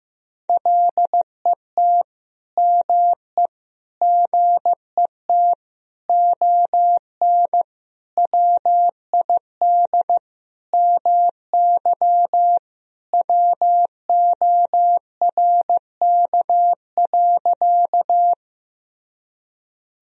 Morse code, heard as a keyed tone, about 15 wpm, 700 Hz. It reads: LET ME GET ON WID MY WORK.
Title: Sentences from Top 300 Words and Abbreviations 15wpm